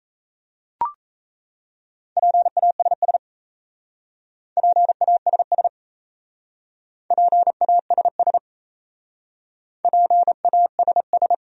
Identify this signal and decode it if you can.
Morse code, heard as a keyed tone, about 37 wpm, 690 Hz. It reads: PASS PASS PASS PASS